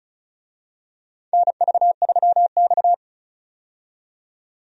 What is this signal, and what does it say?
Morse code, heard as a keyed tone, about 34 wpm, 700 Hz. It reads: NV3X
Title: USA Call Signs - 4 Characters - 35wpm